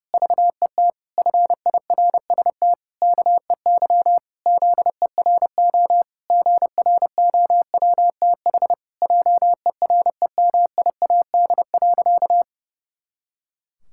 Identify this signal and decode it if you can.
Morse code, heard as a keyed tone, about 30 wpm, 700 Hz. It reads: VET FIRST KEY ZERO GROWTH JEREMIAD.
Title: Pangram Challenge v2 - 30wpm